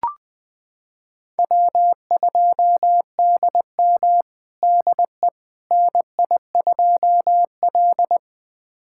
Morse code, heard as a keyed tone, about 20 wpm, 700 Hz. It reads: W2DM DE NI2L
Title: USA Callsign DE Callsign - 4 Characters - 20wpm